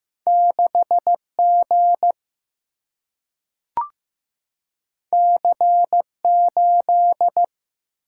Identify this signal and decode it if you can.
Morse code, heard as a keyed tone, about 15 wpm, 700 Hz. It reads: 6 G C 8
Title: Single-Letter Single-Number 15wpm